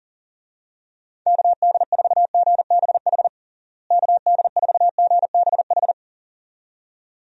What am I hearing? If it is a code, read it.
Morse code, heard as a keyed tone, about 40 wpm, 700 Hz. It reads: KD4GBH KD4GBH
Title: USA Call Signs - 6 Characters - Repeated 2x - 40wpm